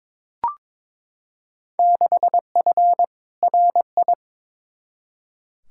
Morse code, heard as a keyed tone, about 22 wpm, 700 Hz. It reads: 6F RI
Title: Field Day Exchanges Encore - 22wpm